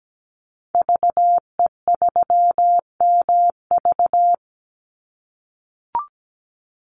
Morse code, heard as a keyed tone, about 17 wpm, 700 Hz. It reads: VE3MV
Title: Call Signs - 17wpm